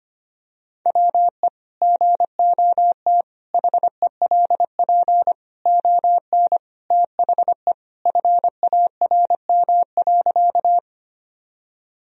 Morse code, heard as a keyed tone, about 25 wpm, 700 Hz. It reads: WE GOT HELP ON THE FARM.
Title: Sentences from Top 1000 Words - Review Part 2 - 25wpm